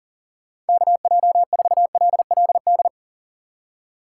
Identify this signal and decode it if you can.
Morse code, heard as a keyed tone, about 52 wpm, 700 Hz. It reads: KJ4LLD